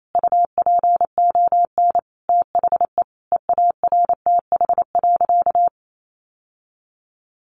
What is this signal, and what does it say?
Morse code, heard as a keyed tone, about 28 wpm, 700 Hz. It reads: UPON THE EARTH.
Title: Sentences from Top 1200 Words 28wpm